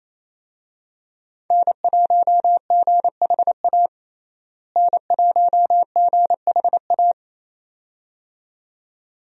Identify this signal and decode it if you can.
Morse code, heard as a keyed tone, about 27 wpm, 700 Hz. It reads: N1GHA N1GHA
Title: USA Call Signs - 5 Characters - Repeated 2x - 28wpm